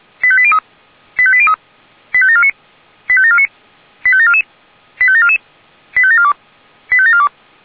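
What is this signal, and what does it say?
Five-tone selective call (ZVEI), heard as quick runs of five tones, tones of about 70 ms. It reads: 85803 85803 85749 85749 85744 85744 85742 85742
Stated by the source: Received by SDR, ZVEI five-tone selective call.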